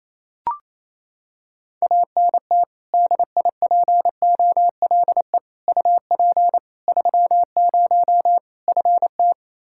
Morse code, heard as a keyed tone, about 27 wpm, 700 Hz. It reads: ANT DIPOLE UP 30 FT
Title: K1USN SST QSO Practice 28wpm